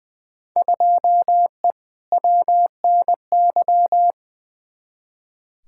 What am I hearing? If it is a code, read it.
Morse code, heard as a keyed tone, about 20 wpm, 700 Hz. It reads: 2E WNY
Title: Field Day Exchanges Encore - 20wpm